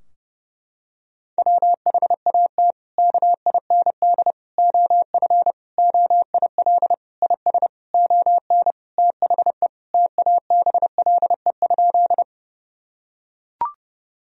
Morse code, heard as a keyed tone, about 30 wpm, 700 Hz. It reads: WHAT KIND OF OIL IS ON THE TABLE?
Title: Sentences from Top 1000 Words - Review Part 4 - 30wpm